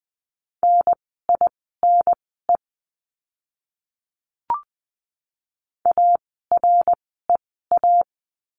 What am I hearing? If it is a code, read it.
Morse code, heard as a keyed tone, about 20 wpm, spaced as for 15 wpm, 700 Hz. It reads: NINE AREA
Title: 061 Lesson 11f - Letter h - Words 20wpm